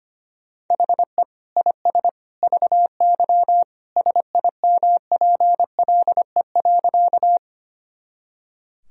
Morse code, heard as a keyed tone, about 25 wpm, 700 Hz. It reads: HE IS VY SIMPLE.
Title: Sentences from Top 400 Words and Abbreviations 25wpm